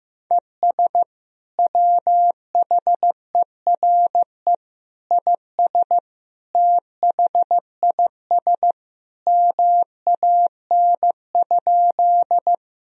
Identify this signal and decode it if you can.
Morse code, heard as a keyed tone, about 15 wpm, 700 Hz. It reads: ES WHERE IS THIS MAN?
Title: Sentences from Top 200 Words and Abbreviations 15wpm